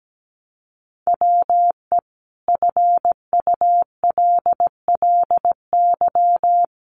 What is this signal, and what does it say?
Morse code, heard as a keyed tone, about 17 wpm, 700 Hz. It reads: WE FULLY
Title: Sentences from Top 1300 Words 17wpm